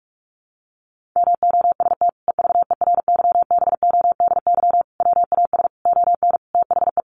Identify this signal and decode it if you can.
Morse code, heard as a keyed tone, about 44 wpm, 700 Hz. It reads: MOST EVERYBODY WAS ON THE
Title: Sentences from Top 1400 Words 45wpm